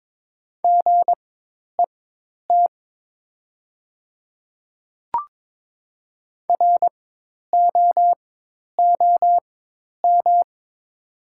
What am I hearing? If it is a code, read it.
Morse code, heard as a keyed tone, about 22 wpm, spaced as for 10 wpm, 700 Hz. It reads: GET ROOM
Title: 4x Character Spacing - Top 300 Words - 22wpm